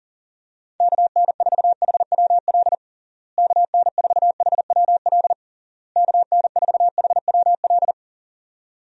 Morse code, heard as a keyed tone, about 37 wpm, 700 Hz. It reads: KN4HWL KN4HWL KN4HWL
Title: USA Call Signs - 6 Characters - Repeated 3x - 40wpm